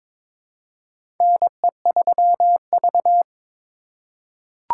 Morse code, heard as a keyed tone, about 22 wpm, 700 Hz. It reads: NE3V
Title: USA Call Signs - 3x Character Spacing - 22wpm